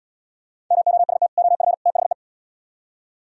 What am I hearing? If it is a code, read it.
Morse code, heard as a keyed tone, about 77 wpm, 700 Hz. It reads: MOST OF THE